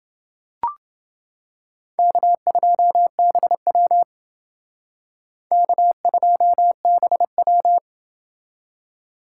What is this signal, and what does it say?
Morse code, heard as a keyed tone, about 28 wpm, 700 Hz. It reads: K2BW K2BW